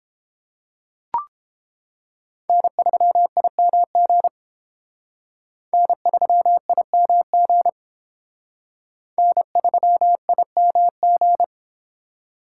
Morse code, heard as a keyed tone, about 31 wpm, 700 Hz. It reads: N3IMG N3IMG N3IMG